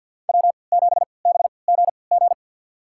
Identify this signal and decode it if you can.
Morse code, heard as a keyed tone, about 52 wpm, spaced as for 29 wpm, 690 Hz. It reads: W7BGG
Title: USA Call Signs - 5 Characters - 3x Character Spacing - 50wpm